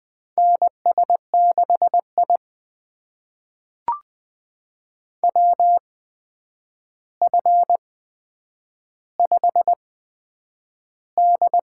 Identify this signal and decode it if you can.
Morse code, heard as a keyed tone, about 20 wpm, spaced as for 9 wpm, 700 Hz. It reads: NS6I WF5D
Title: USA Call Signs - 4 Characters - 8x Character Spacing - 20wpm